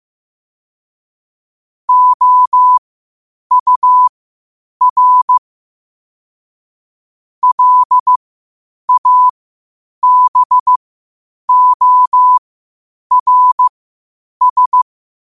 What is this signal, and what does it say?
Morse code, heard as a keyed tone, about 15 wpm, spaced as for 8 wpm, 1000 Hz. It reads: OUR LABORS
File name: qotdfiles / 8wpm